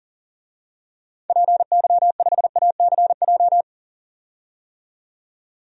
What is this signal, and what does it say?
Morse code, heard as a keyed tone, about 40 wpm, 700 Hz. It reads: PY5ACJ
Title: DXCC-100 Call Signs - 6 Characters - 40wpm